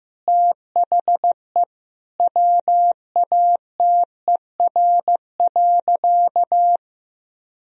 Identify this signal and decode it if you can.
Morse code, heard as a keyed tone, about 16 wpm, 700 Hz. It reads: THE WATER.
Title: Sentences from Top 500 Words and Abbreviations 15wpm